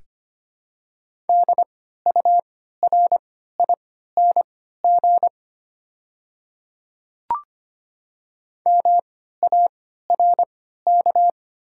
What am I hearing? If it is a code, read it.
Morse code, heard as a keyed tone, about 25 wpm, spaced as for 14 wpm, 700 Hz. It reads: DURING MARK
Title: 3x Character Spacing - Top 500 Words - 25wpm